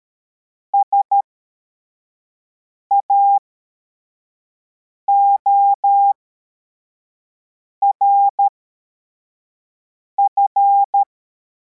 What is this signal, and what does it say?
Morse code, heard as a keyed tone, about 13 wpm, spaced as for 4 wpm, 800 Hz. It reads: SAORF
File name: formation / cw / Text71